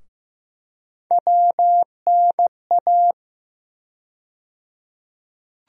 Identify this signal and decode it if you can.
Morse code, heard as a keyed tone, about 16 wpm, 700 Hz. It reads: WNA
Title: USA Call Sign Postfixes - 3 Letters 15wpm